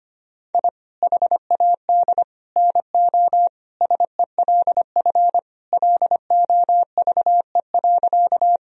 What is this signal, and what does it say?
Morse code, heard as a keyed tone, about 25 wpm, 700 Hz. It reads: I HAD NO SELF LOVE.
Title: Sentences from Top 1000 Words - Review Part 1 - 25wpm